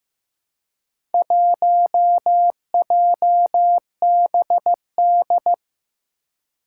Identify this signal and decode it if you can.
Morse code, heard as a keyed tone, about 15 wpm, 700 Hz. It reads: 1JBD
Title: Four Challenging Letter-Number Combinations 15wpm